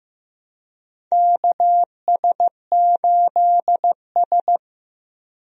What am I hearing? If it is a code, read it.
Morse code, heard as a keyed tone, about 15 wpm, 700 Hz. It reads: KS8S
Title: USA Call Signs - 4 Characters - Repeated 3x - 15wpm